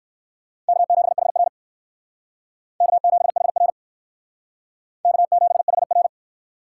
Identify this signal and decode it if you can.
Morse code, heard as a keyed tone, about 55 wpm, 700 Hz. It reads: K7HR K7HR K7HR
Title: Speed Racing - USA 1x2 Call Signs 45wpm